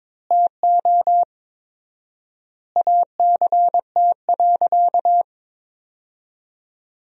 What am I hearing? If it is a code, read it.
Morse code, heard as a keyed tone, about 22 wpm, 700 Hz. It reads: TO ACT.
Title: 4x Word-Spacing - Sentences from Top 200 Words - 22wpm